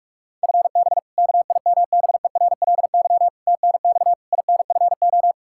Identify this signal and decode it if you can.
Morse code, heard as a keyed tone, about 43 wpm, 700 Hz. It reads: FB KIMBERLY TNX INFO
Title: Practice QSO - Exchange 3 - More Info - 45wpm